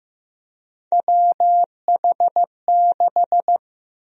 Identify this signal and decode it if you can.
Morse code, heard as a keyed tone, about 15 wpm, 700 Hz. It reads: WH6
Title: USA Call Sign Prefixes - Repeated 2x - 15wpm